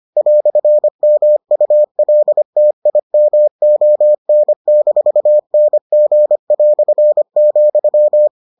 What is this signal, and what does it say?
Morse code, heard as a keyed tone, about 25 wpm, 600 Hz. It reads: "MULTIMON-NG",